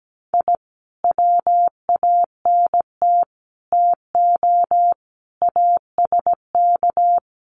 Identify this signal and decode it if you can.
Morse code, heard as a keyed tone, about 17 wpm, 700 Hz. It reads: I WANT TO ASK
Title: Sets of 5 Words 17wpm